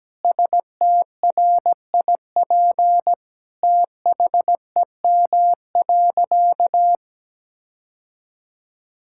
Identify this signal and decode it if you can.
Morse code, heard as a keyed tone, about 17 wpm, 700 Hz. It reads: STRIP THEM.
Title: Sentences from Top 1600 Words 17wpm